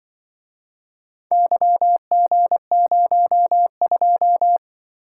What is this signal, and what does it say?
Morse code, heard as a keyed tone, about 24 wpm, 700 Hz. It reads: YG02
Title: Speed-Racing - Four Letter-Number Combinations - 20wpm